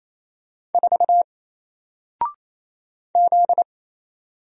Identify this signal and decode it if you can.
Morse code, heard as a keyed tone, about 27 wpm, 700 Hz. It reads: 4 Z
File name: Single-Letter-Number-28wpm